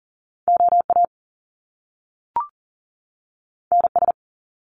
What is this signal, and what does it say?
Morse code, heard as a keyed tone, about 38 wpm, 700 Hz. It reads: OA NS
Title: Top Two Letter Combinations in Words 40wpm